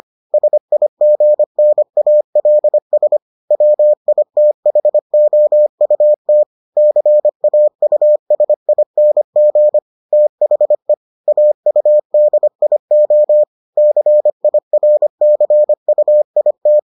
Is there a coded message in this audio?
Yes: Morse code from 0.3 to 16.8 s, heard as a keyed tone, about 25 wpm, 600 Hz. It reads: SIGNALS WITHOUT CAUSING THE AUDIO CIRCUIT